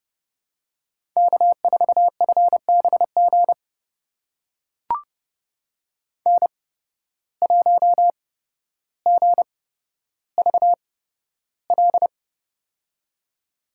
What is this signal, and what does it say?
Morse code, heard as a keyed tone, about 30 wpm, spaced as for 13 wpm, 700 Hz. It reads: K4FBG N1GVL
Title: USA Call Signs - 5 Characters - 8x Character Spacing - 30wpm